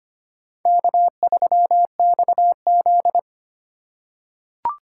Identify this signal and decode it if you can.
Morse code, heard as a keyed tone, about 25 wpm, 700 Hz. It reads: K3XZ
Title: USA Call Signs - 4 Characters - 4x Character Spacing - 25wpm